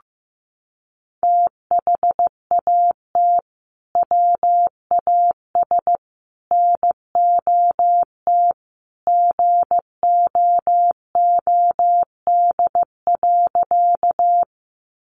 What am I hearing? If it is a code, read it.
Morse code, heard as a keyed tone, about 15 wpm, 700 Hz. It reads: THAT WAS NOT GOOD.
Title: Sentences from Top 600 Words 15wpm